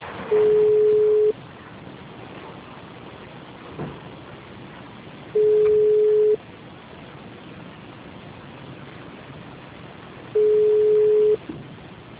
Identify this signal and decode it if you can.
Ringback tone (Europe, 425 Hz).